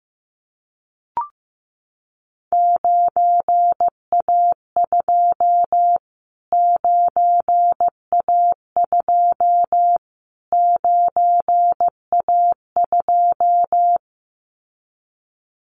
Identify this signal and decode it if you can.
Morse code, heard as a keyed tone, about 15 wpm, 700 Hz. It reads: 9A2 9A2 9A2
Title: Callsign Prefixes for Top 25 DXCC - Repeated 3x - 15wpm